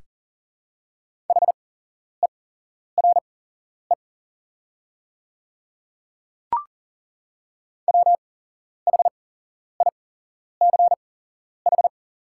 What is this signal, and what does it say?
Morse code, heard as a keyed tone, about 40 wpm, spaced as for 11 wpm, 700 Hz. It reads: HERE WHICH
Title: 8x Character Spacing - Top 200 Words - 40wpm